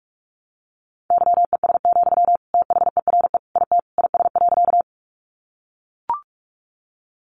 Morse code, heard as a keyed tone, about 44 wpm, 700 Hz. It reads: YES, THERE IT IS.
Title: Sentences from Top 500 Words 45wpm